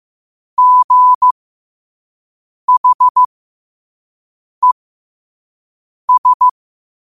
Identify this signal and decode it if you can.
Morse code, heard as a keyed tone, about 15 wpm, spaced as for 5 wpm, 1000 Hz. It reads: GHES
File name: qotdfiles / 5wpm